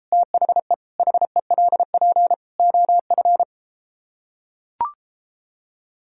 Morse code, heard as a keyed tone, about 33 wpm, 700 Hz. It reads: THE HELP OF